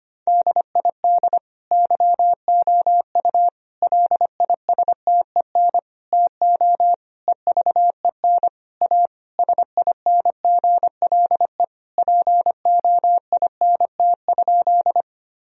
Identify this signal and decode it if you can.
Morse code, heard as a keyed tone, about 25 wpm, 700 Hz. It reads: DID YOU LISTEN TO EVEN A SINGLE POINT?
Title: Sentences from Top 1000 Words - Review Part 4 - 25wpm